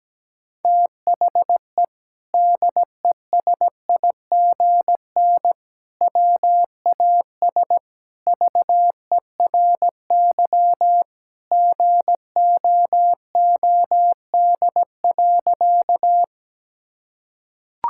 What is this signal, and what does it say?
Morse code, heard as a keyed tone, about 17 wpm, 700 Hz. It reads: THE DESIGN WAS VERY GOOD.
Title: Sentences from Top 1000 Words - Review Part 5 - 17wpm